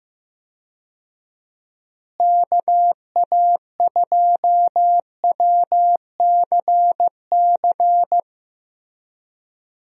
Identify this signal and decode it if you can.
Morse code, heard as a keyed tone, about 15 wpm, 700 Hz. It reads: KA2WCC
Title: USA Call Signs - 6 Characters - Repeated 3x - 15wpm